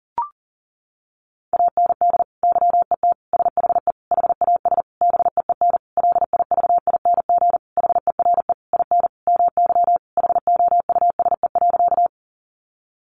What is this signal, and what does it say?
Morse code, heard as a keyed tone, about 37 wpm, 700 Hz. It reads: AND YET SHE HAS BEEN LIVING HERE IN MY HOUSE.
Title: Sentences from Top 1100 Words 40wpm